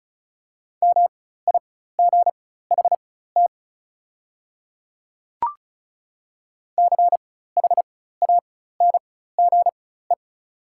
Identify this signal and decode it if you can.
Morse code, heard as a keyed tone, about 34 wpm, spaced as for 16 wpm, 700 Hz. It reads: MIGHT CHANGE
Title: 4x Character Spacing - Top 300 Words - 35wpm